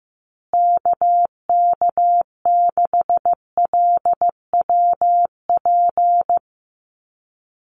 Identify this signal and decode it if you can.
Morse code, heard as a keyed tone, about 15 wpm, 700 Hz. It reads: KK6LWP
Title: USA Call Signs Repeated 2x 15wpm